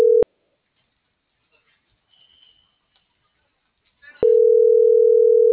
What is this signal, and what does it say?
Ringback tone (North America), 440 + 480 Hz.